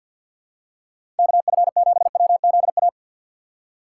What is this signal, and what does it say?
Morse code, heard as a keyed tone, about 52 wpm, 700 Hz. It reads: KF7WZA
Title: USA Call Signs - 6 Characters - 4x Character Spacing - 50wpm